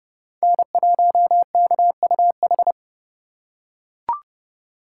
Morse code, heard as a keyed tone, about 28 wpm, 700 Hz. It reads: N1KUH